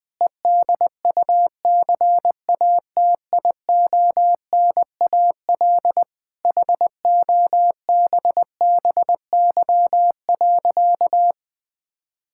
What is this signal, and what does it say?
Morse code, heard as a keyed tone, about 20 wpm, 700 Hz. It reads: EDUCATIONAL HOBBY.
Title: Difficult Sentences with Hints - 20wpm